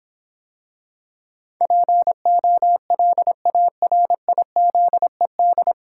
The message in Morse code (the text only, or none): POLARIZED